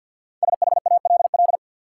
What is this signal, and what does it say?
Morse code, heard as a keyed tone, about 52 wpm, 700 Hz. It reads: SHALL
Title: Five Letter Words - 50wpm